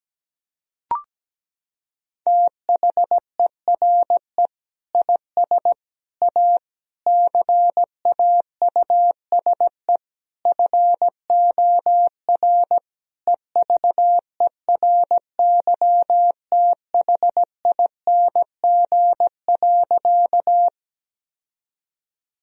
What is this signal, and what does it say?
Morse code, heard as a keyed tone, about 17 wpm, 700 Hz. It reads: THERE IS A CAUSE FOR EVERYTHING.
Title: Sentences from Top 1000 Words - Review Part 1 - 17wpm